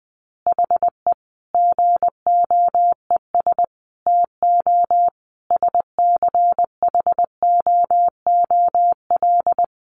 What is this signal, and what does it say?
Morse code, heard as a keyed tone, about 20 wpm, 700 Hz. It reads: HE GOES TO SCHOOL
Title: Additional Easy Phrases 20wpm